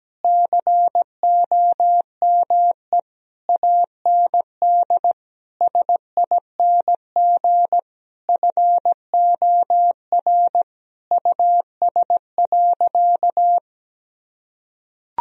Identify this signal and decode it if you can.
Morse code, heard as a keyed tone, about 17 wpm, 700 Hz. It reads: COME AND SING FOR US.
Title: Sentences from Top 1000 Words - Review Part 2 - 17wpm